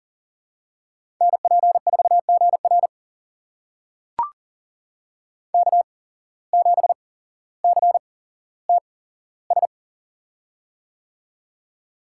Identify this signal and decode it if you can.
Morse code, heard as a keyed tone, about 40 wpm, spaced as for 17 wpm, 700 Hz. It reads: NP4GR K7CTS
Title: USA Call Signs - 5 Characters - 8x Character Spacing - 40wpm